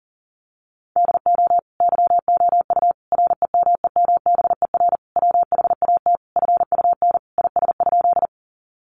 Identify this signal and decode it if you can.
Morse code, heard as a keyed tone, about 37 wpm, 700 Hz. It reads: DO YOU REMEMBER WHAT FUN IS?
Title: Sentences from Top 700 Words 40wpm